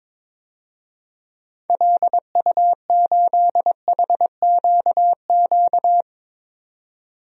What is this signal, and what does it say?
Morse code, heard as a keyed tone, about 21 wpm, 700 Hz. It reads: LU8HQQ
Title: DXCC-100 Call Signs - 6 Characters - 22wpm